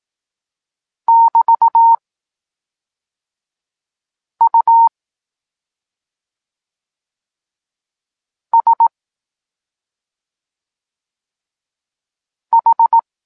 Morse code, heard as a keyed tone, about 18 wpm, spaced as for 2 wpm, 920 Hz. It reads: <BT> USH
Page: Tekst 8 – doskonalenie odbioru poznanych liter